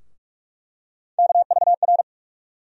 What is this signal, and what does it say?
Morse code, heard as a keyed tone, about 46 wpm, 700 Hz. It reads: KVR
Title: USA Call Sign Postfixes - 3 Letters 45wpm